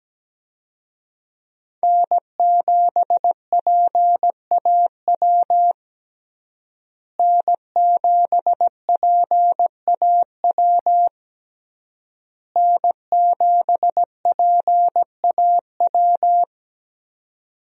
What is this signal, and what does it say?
Morse code, heard as a keyed tone, about 17 wpm, 700 Hz. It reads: N7PAW N7PAW N7PAW
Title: USA Call Signs - 5 Characters - Repeated 3x - 17wpm